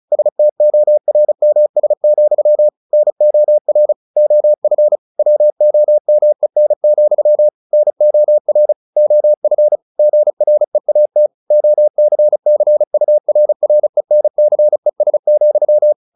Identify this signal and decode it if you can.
Morse code, heard as a keyed tone, about 34 wpm, 600 Hz. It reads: STORMS, NOR OF WOMEN, NOR OF GREAT OCCURRENCES,
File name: the old man and the sea / the old man and the sea 1 35wpm